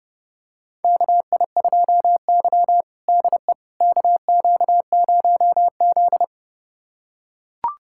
Morse code, heard as a keyed tone, about 30 wpm, 700 Hz. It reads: KI2Y DE KQ0Z
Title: USA Callsign DE Callsign - 4 Characters - 30wpm